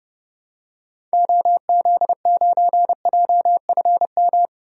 Morse code, heard as a keyed tone, about 31 wpm, 700 Hz. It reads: OZ9JFM
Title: DXCC-100 Call Signs - 6 Characters - Repeated 2x - 30wpm